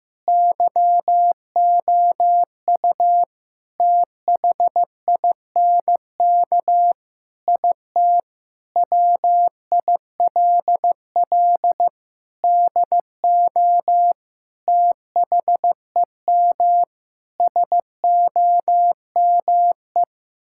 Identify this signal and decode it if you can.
Morse code, heard as a keyed tone, about 15 wpm, 700 Hz. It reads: YOU THINK IT WILL DO THEM SOME
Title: Sentences from Top 1000 Words - Review Part 6 - 15wpm